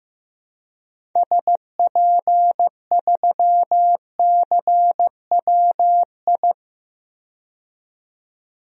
Morse code, heard as a keyed tone, about 15 wpm, 700 Hz. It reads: SP3CWI
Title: DXCC-100 Call Signs - 6 Characters - 15wpm